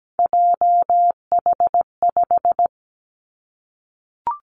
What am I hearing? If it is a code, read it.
Morse code, heard as a keyed tone, about 17 wpm, 700 Hz. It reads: JH5
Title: Three Challenging Letter-Number Combinations 17wpm